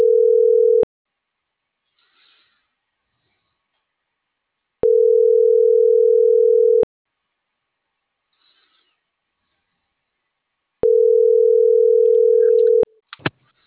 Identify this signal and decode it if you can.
Ringback tone (North America), 440 + 480 Hz.